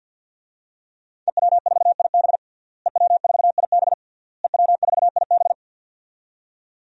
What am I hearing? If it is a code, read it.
Morse code, heard as a keyed tone, about 52 wpm, 700 Hz. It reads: EW4IB EW4IB EW4IB
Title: DXCC-100 Call Signs - 5 Characters - Repeated 3x - 50wpm